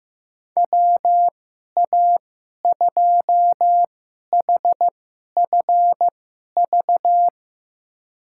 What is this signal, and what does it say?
Morse code, heard as a keyed tone, about 15 wpm, spaced as for 11 wpm, 700 Hz. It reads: WA2HFV
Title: USA Call Signs - 2x Character Spacing - 15wpm